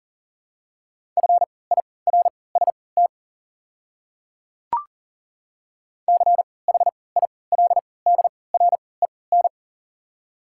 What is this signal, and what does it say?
Morse code, heard as a keyed tone, about 40 wpm, spaced as for 23 wpm, 700 Hz. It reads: FIRST CHILDREN
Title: 3x Character Spacing - Top 300 Words - 40wpm